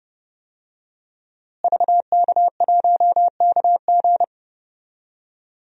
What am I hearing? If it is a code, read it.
Morse code, heard as a keyed tone, about 30 wpm, 700 Hz. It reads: VK1KG
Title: DXCC-100 Call Signs - 5 Characters - Repeated 2x - 30wpm